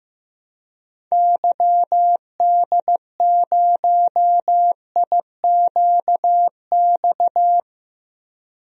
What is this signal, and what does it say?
Morse code, heard as a keyed tone, about 15 wpm, 700 Hz. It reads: YD0IQX
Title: DXCC-100 Call Signs - 6 Characters - 15wpm